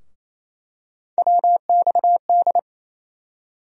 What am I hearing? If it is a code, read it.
Morse code, heard as a keyed tone, about 28 wpm, 700 Hz. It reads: WXD
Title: USA Call Sign Postfixes - 3 Letters 28wpm